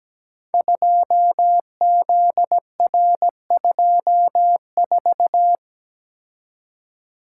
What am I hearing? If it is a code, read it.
Morse code, heard as a keyed tone, about 17 wpm, 700 Hz. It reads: 2ZR24
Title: Code Groups - 5 Letter:Numbers - 17wpm